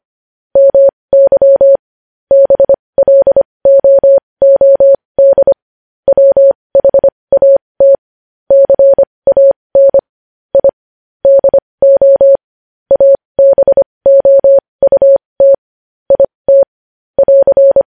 Morse code, heard as a keyed tone, about 25 wpm, spaced as for 20 wpm, 550 Hz. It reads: MY BLOOD WHAT CAN I DO ABOUT IT <AR>
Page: News Headlines in Morse Code at 20 WPM